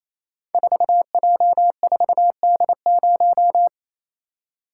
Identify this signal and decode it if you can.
Morse code, heard as a keyed tone, about 28 wpm, 700 Hz. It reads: 4J4D0
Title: Code Groups - 5 Letter:Numbers - 28wpm